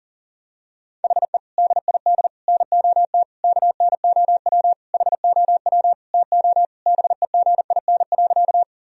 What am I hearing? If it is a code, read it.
Morse code, heard as a keyed tone, about 37 wpm, 700 Hz. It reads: HE DID NOT KNOW HOW TO BEGIN.
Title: Sentences from Top 1000 Words - Review Part 1 - 40wpm